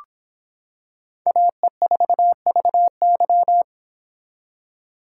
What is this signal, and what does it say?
Morse code, heard as a keyed tone, about 25 wpm, 700 Hz. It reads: AE4VY